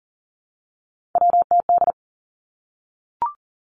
Morse code, heard as a keyed tone, about 42 wpm, 700 Hz. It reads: WTD
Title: Three Challenging Letter-Number Combinations 40wpm